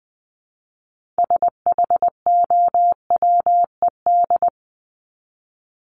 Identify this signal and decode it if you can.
Morse code, heard as a keyed tone, about 20 wpm, 700 Hz. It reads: SHOWED
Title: ED Suffix - Top 100 Words with Base Words 20wpm